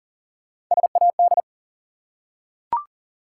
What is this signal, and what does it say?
Morse code, heard as a keyed tone, about 40 wpm, 700 Hz. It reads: SAD
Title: Three Letter Words - 40wpm